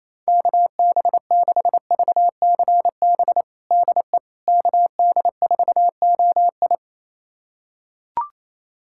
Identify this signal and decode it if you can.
Morse code, heard as a keyed tone, about 28 wpm, 700 Hz. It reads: KB6VCB DE KD4OI
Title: USA Callsign DE Callsign 28wpm